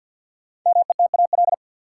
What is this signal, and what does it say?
Morse code, heard as a keyed tone, about 51 wpm, 700 Hz. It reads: METAL
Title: Sentences from Top 1000 Words - Review Part 6 - New Word Drill - 50wpm